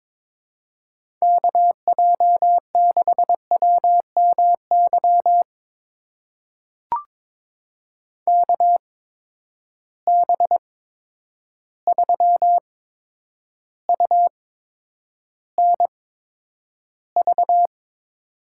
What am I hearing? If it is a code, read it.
Morse code, heard as a keyed tone, about 21 wpm, spaced as for 9 wpm, 700 Hz. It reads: KJ6WMY KB3UNV
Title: USA Call Signs - 6 Characters - 8x Character Spacing - 22wpm